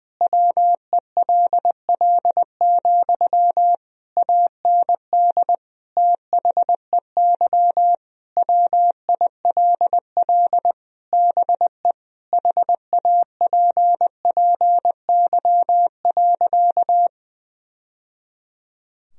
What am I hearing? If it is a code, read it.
Morse code, heard as a keyed tone, about 20 wpm, 700 Hz. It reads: WELL, AND THEY WILL BE HAPPY.
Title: Sentences from Top 1000 Words - Review Part 2 - 20wpm